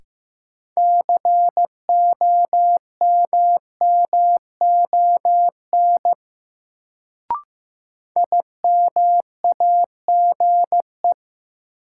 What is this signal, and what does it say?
Morse code, heard as a keyed tone, about 15 wpm, 700 Hz. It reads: COMMON IMAGE
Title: Sentences from Top 700 Words - New Word Drill - 15wpm